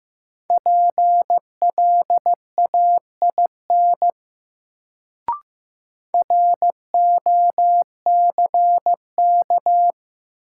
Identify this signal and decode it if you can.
Morse code, heard as a keyed tone, about 15 wpm, 700 Hz. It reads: PLAIN ROCK
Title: Sentences from Top 400 Words - New Word Drill - 15wpm